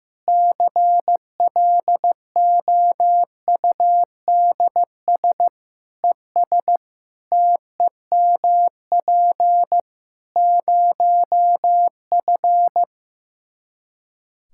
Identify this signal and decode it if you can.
Morse code, heard as a keyed tone, about 15 wpm, 700 Hz. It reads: CLOUDS ES TEMP 0F
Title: Practice QSO - Exchange 2 - Info - 15wpm